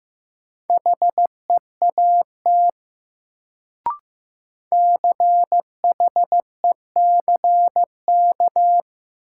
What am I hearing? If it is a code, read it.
Morse code, heard as a keyed tone, about 15 wpm, 700 Hz. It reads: HEAT CHECK
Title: Sentences from Top 1000 Words - Review Part 3 - New Word Drill - 15wpm